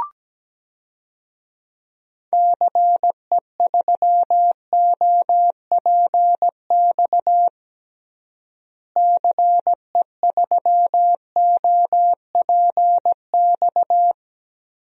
Morse code, heard as a keyed tone, about 17 wpm, 700 Hz. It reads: CE3OPX CE3OPX
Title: DXCC-100 Call Signs - 6 Characters - Repeated 2x - 17wpm